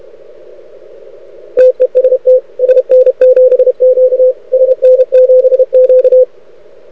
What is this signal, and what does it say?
Morse code, heard as a keyed tone, about 31 wpm, 500 Hz. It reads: TEST SN7Q SN7Q
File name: rec contest / arrl cw 2018 / test 32